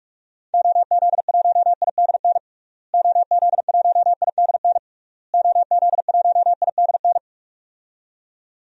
Morse code, heard as a keyed tone, about 46 wpm, 700 Hz. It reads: OZ1IDN OZ1IDN OZ1IDN
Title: DXCC-100 Call Signs - 6 Characters - Repeated 3x - 45wpm